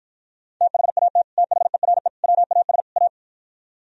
Morse code, heard as a keyed tone, about 53 wpm, 700 Hz. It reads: THAT THERE WAS A